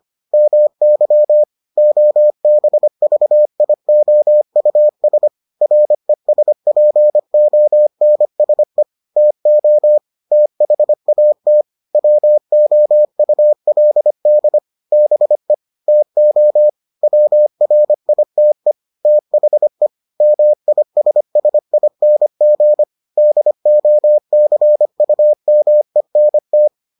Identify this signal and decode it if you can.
Morse code, heard as a keyed tone, about 25 wpm, 600 Hz. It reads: MY OBVIOUS RESPONSE TO THAT WOULD BE TO WRITE THE MISSING DOCUMENT